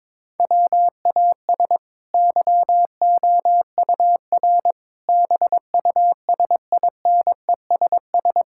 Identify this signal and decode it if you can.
Morse code, heard as a keyed tone, about 22 wpm, 700 Hz. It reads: WAS YOUR BUSINESS
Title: Sentences from Top 1000 Words - Review Part 5 - 22wpm